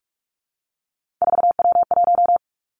Morse code, heard as a keyed tone, about 46 wpm, 700 Hz. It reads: 4W1
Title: Three Challenging Letter-Number Combinations 45wpm